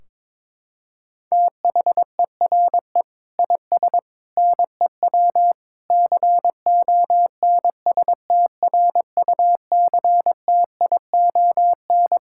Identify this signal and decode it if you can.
Morse code, heard as a keyed tone, about 22 wpm, 700 Hz. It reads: THERE IS NEW CONSTRUCTION
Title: Sentences from Top 1600 Words 22wpm